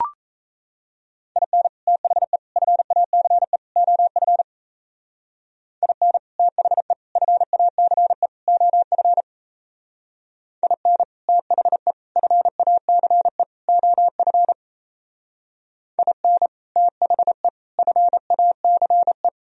Morse code, heard as a keyed tone, about 34 wpm, 700 Hz. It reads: IN THE FACE OF IN THE FACE OF IN THE FACE OF IN THE FACE
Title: Speed-Racing - Sets of 4 Words 28wpm